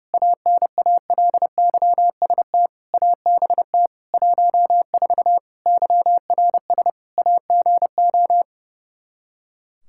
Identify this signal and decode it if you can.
Morse code, heard as a keyed tone, about 32 wpm, 700 Hz. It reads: ANALYST ABT 14 YRS AGO
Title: Practice QSO - Exchange 3 - More Info - 30wpm